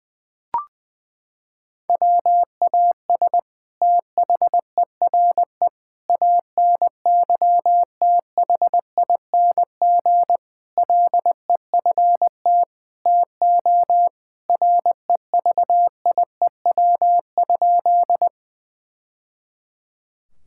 Morse code, heard as a keyed tone, about 20 wpm, 700 Hz. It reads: WAS THERE ANYTHING LEFT TO REVIEW?
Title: Sentences from Top 1000 Words - Review Part 6 - 20wpm